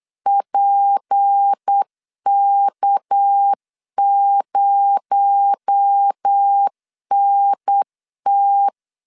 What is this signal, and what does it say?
Morse code, heard as a keyed tone, about 8 wpm, 800 Hz. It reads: PK0NT
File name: LEZIONI / 6 PKX0 / 6 PKX0 8wpm 3341